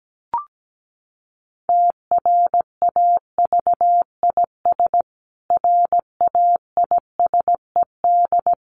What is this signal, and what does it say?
Morse code, heard as a keyed tone, about 17 wpm, 700 Hz. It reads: TRAVIS RAISED
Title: A Sound Of Thunder 17wpm